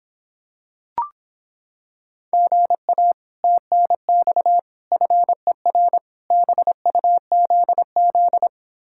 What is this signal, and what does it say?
Morse code, heard as a keyed tone, about 26 wpm, 700 Hz. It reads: GA TNX FER BUZZ